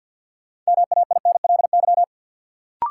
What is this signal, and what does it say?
Morse code, heard as a keyed tone, about 53 wpm, 700 Hz. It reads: MAINLY
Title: Sentences from Top 1400 Words - New Word Drill - 50wpm